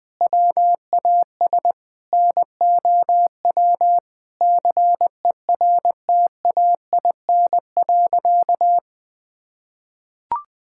Morse code, heard as a keyed tone, about 20 wpm, 700 Hz. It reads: WAS NOW CERTAIN.
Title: Sentences from Top 1000 Words - Review Part 5 - 20wpm